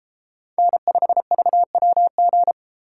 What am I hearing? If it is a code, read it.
Morse code, heard as a keyed tone, about 29 wpm, 700 Hz. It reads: N5VWG